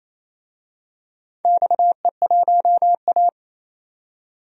Morse code, heard as a keyed tone, about 28 wpm, 700 Hz. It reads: XE1A
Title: DXCC-100 Call Signs - 4 Characters - 28wpm